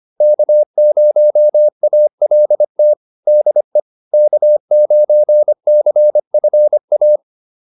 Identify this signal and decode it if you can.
Morse code, heard as a keyed tone, about 25 wpm, 600 Hz. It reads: K0ALT DE K9CFA
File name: QSO202 25